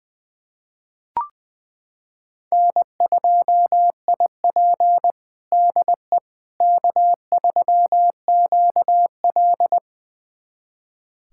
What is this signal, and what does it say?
Morse code, heard as a keyed tone, about 20 wpm, 700 Hz. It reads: N2IP DE K3QL
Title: USA Callsign DE Callsign - 4 Characters - 20wpm